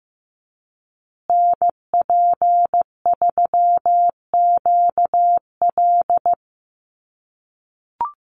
Morse code, heard as a keyed tone, about 15 wpm, 700 Hz. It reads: NP3QL
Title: USA Call Signs Repeated 3x 15wpm